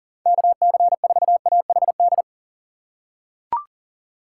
Morse code, heard as a keyed tone, about 40 wpm, 700 Hz. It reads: KC4AHD
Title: USA Call Signs - 6x Character Spacing - 40wpm